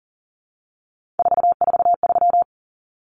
Morse code, heard as a keyed tone, about 40 wpm, 700 Hz. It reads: 443
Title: Common Cut Numbers - 3-Digit 40wpm